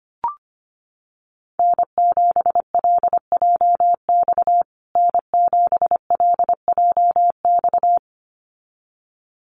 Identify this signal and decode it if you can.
Morse code, heard as a keyed tone, about 25 wpm, 700 Hz. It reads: N7LJX N7LJX
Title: USA Call Signs Repeated 2x 25wpm